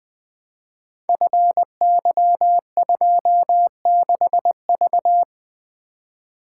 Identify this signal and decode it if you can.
Morse code, heard as a keyed tone, about 20 wpm, 700 Hz. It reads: FY26V
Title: Code Groups - 5 Letter:Numbers - 20wpm